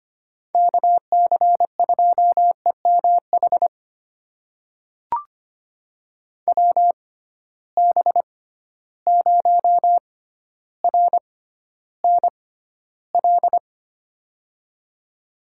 Morse code, heard as a keyed tone, about 25 wpm, spaced as for 13 wpm, 700 Hz. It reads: KC2EMH WB0RNL
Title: USA Call Signs - 6 Characters - 6x Character Spacing - 25wpm